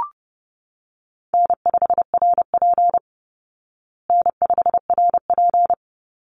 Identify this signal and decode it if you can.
Morse code, heard as a keyed tone, about 30 wpm, 700 Hz. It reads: N5RP N5RP
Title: Easy Call Signs Repeated 2x - 30wpm